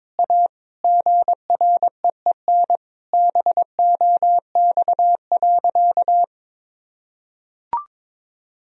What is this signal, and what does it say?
Morse code, heard as a keyed tone, about 21 wpm, 700 Hz. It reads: A GREEN BOX.
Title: Sentences from Top 1000 Words - Review Part 2 - 22wpm